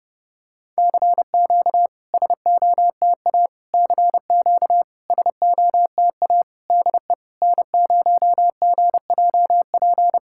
Morse code, heard as a keyed tone, about 30 wpm, 700 Hz. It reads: CQ SOTA CQ SOTA DE N0GJP
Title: SOTA QSO Practice 30wpm